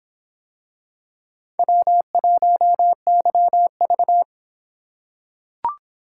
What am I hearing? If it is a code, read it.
Morse code, heard as a keyed tone, about 26 wpm, 700 Hz. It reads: W1YV